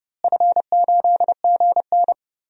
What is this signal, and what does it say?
Morse code, heard as a keyed tone, about 30 wpm, 700 Hz. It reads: F8GN
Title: DXCC-100 Call Signs - 4 Characters - Repeated 3x - 30wpm